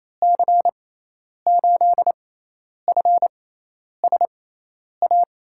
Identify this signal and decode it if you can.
Morse code, heard as a keyed tone, about 28 wpm, spaced as for 10 wpm, 700 Hz. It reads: C8FSA
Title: USA Call Signs - 6 Characters - 6x Character Spacing - 28wpm